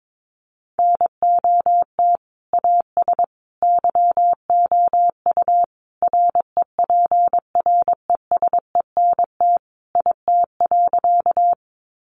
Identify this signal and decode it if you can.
Morse code, heard as a keyed tone, about 22 wpm, 700 Hz. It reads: NOT AS YOU REPRESENT IT.
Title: Sentences from Top 1200 Words 22wpm